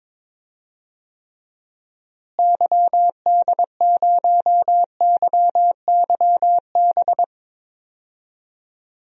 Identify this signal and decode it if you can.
Morse code, heard as a keyed tone, about 21 wpm, 700 Hz. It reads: YD0YYB
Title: DXCC-100 Call Signs - 6 Characters - Repeated 3x - 22wpm